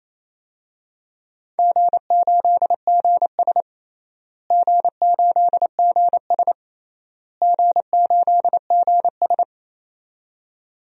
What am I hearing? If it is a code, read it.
Morse code, heard as a keyed tone, about 27 wpm, 700 Hz. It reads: G8GS G8GS G8GS
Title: DXCC-100 Call Signs - 4 Characters - Repeated 3x - 28wpm